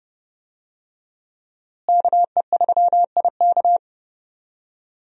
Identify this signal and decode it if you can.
Morse code, heard as a keyed tone, about 30 wpm, 700 Hz. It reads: KE3IK
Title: USA Call Signs - 5 Characters - Repeated 3x - 30wpm